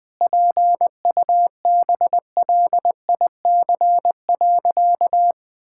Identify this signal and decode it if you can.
Morse code, heard as a keyed tone, about 20 wpm, 700 Hz. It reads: PUBLIC.
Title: Sentences with 7-Word Phrases - 20wpm